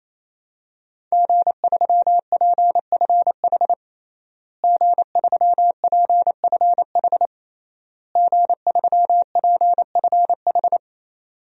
Morse code, heard as a keyed tone, about 27 wpm, 700 Hz. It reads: G3PFH G3PFH G3PFH
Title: DXCC-100 Call Signs - 5 Characters - Repeated 3x - 28wpm